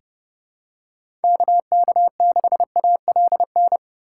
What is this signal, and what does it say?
Morse code, heard as a keyed tone, about 30 wpm, 700 Hz. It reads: KK6ALN
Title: USA Call Signs - 6 Characters - 3x Character Spacing - 30wpm